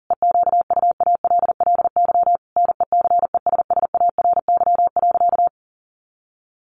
Morse code, heard as a keyed tone, about 40 wpm, 700 Hz. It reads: EQUALLY NECESSARY.
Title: Sentences from Top 1400 Words 40wpm